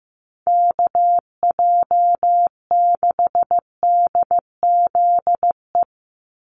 Morse code, heard as a keyed tone, about 15 wpm, 700 Hz. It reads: KJ6DZE
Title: USA Call Signs 15wpm